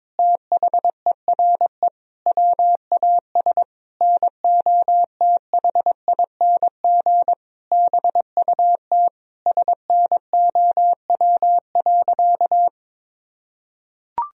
Morse code, heard as a keyed tone, about 22 wpm, 700 Hz. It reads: THERE WAS NOTHING BUT SNOW.
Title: Sentences from Top 1000 Words - Review Part 3 - 22wpm